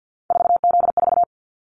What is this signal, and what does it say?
Morse code, heard as a keyed tone, about 50 wpm, 700 Hz. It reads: 4Z4
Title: Callsign Prefixes for Top 50 DXCC - 50wpm